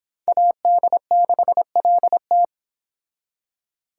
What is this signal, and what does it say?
Morse code, heard as a keyed tone, about 26 wpm, 700 Hz. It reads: AD6LT